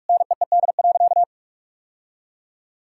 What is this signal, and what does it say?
Morse code, heard as a keyed tone, about 46 wpm, 700 Hz. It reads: NEED.
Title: Sentences from Top 1000 Words - Review Part 5 - 45wpm